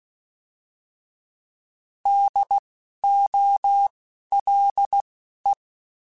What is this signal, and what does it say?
Morse code, heard as a keyed tone, about 16 wpm, spaced as for 12 wpm, 800 Hz. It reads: DOLE
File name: Alphabetical / Alphabetical mp3s / Ds @10